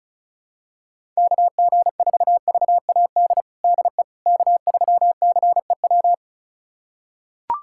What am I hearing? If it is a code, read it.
Morse code, heard as a keyed tone, about 34 wpm, 700 Hz. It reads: KG4VAD DE K3CEW
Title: USA Callsign DE Callsign 35wpm